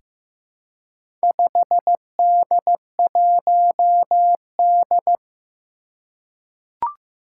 Morse code, heard as a keyed tone, about 15 wpm, 700 Hz. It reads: 5D1D
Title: Four Challenging Letter-Number Combinations 15wpm